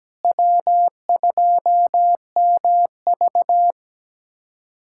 Morse code, heard as a keyed tone, about 17 wpm, 700 Hz. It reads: W2MV
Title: USA Callsign DE Callsign - 4 Characters - 17wpm